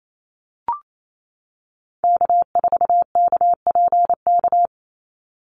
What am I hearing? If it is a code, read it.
Morse code, heard as a keyed tone, about 28 wpm, 700 Hz. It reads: K4KPK
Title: NA SOTA Activator Callsigns 28wpm